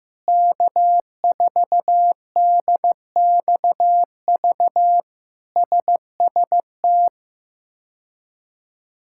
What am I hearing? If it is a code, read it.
Morse code, heard as a keyed tone, about 15 wpm, 700 Hz. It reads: K4DXV SST
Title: K1USN SST QSO Practice - 15wpm